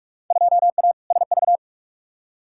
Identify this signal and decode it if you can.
Morse code, heard as a keyed tone, about 46 wpm, 700 Hz. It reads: 2A SV
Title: Field Day Exchanges - 45wpm